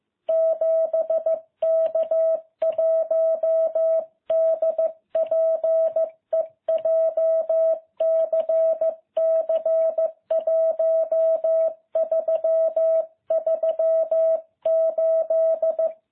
Morse code, heard as a keyed tone, about 15 wpm, 660 Hz. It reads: 7K1DPE JCC 1338